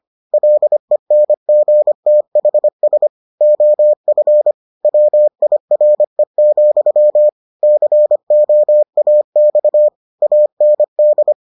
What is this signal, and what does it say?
Morse code, heard as a keyed tone, about 25 wpm, 600 Hz. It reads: LENGTHS OF WIRE, COAX AND